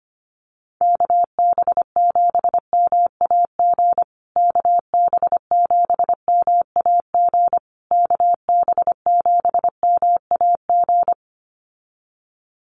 Morse code, heard as a keyed tone, about 25 wpm, 700 Hz. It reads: KB7MAG KB7MAG KB7MAG
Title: USA Call Signs Repeated 3x 25wpm